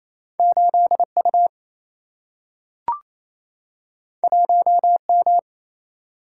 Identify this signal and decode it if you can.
Morse code, heard as a keyed tone, about 28 wpm, 700 Hz. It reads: 8U 1M
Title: Single-Letter Single-Number 28wpm